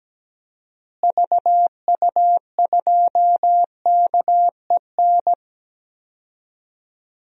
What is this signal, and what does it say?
Morse code, heard as a keyed tone, about 17 wpm, 700 Hz. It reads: VU2KEN
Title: DXCC-100 Call Signs - 6 Characters - Repeated 3x - 17wpm